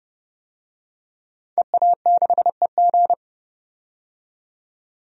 Morse code, heard as a keyed tone, about 30 wpm, 690 Hz. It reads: EA6EG
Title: DXCC-100 Call Signs - 5 Characters - 30wpm